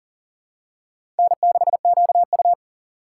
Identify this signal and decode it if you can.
Morse code, heard as a keyed tone, about 40 wpm, 700 Hz. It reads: N6QU
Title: USA Call Signs - 4 Characters - 6x Character Spacing - 40wpm